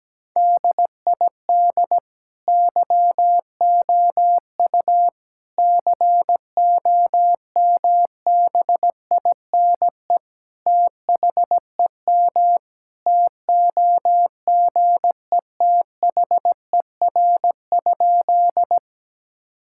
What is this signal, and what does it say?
Morse code, heard as a keyed tone, about 17 wpm, 700 Hz. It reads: DID YOU COMBINE THEM TOGETHER?
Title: Sentences from Top 1600 Words 17wpm